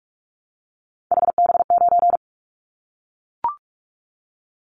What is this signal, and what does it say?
Morse code, heard as a keyed tone, about 44 wpm, 700 Hz. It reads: HB9
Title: Callsign Prefixes for Top 25 DXCC - Repeated 3x - 45wpm